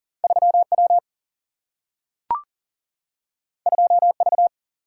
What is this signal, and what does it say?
Morse code, heard as a keyed tone, about 40 wpm, 700 Hz. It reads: 3W 2V
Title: Single-Letter Single-Number 40wpm